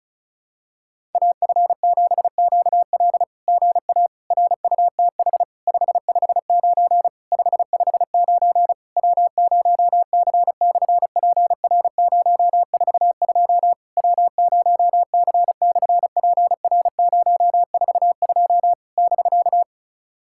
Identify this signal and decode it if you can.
Morse code, heard as a keyed tone, about 36 wpm, 700 Hz. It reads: AF7QL GA RUTH 559 559 W0C/PR042 W0C/PR042 <BK>
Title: SOTA QSO Practice 35wpm